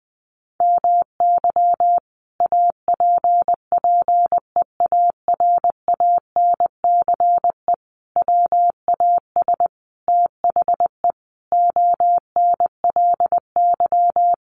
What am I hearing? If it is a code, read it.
Morse code, heard as a keyed tone, about 20 wpm, 700 Hz. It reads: MY APPEARANCE WAS THE ONLY
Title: Sentences from Top 1100 Words 20wpm